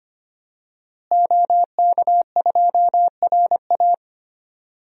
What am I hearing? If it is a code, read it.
Morse code, heard as a keyed tone, about 25 wpm, 700 Hz. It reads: OK2RA
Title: DXCC-100 Call Signs - 5 Characters - Repeated 2x - 25wpm